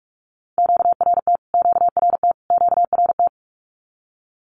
Morse code, heard as a keyed tone, about 46 wpm, 700 Hz. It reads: QRT QRT QRT
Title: Q-Codes Repeated 3x 45wpm